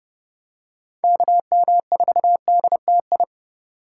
Morse code, heard as a keyed tone, about 29 wpm, 700 Hz. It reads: KM4DTI